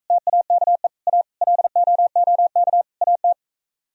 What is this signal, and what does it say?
Morse code, heard as a keyed tone, about 45 wpm, 700 Hz. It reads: TAKE A LOOK AT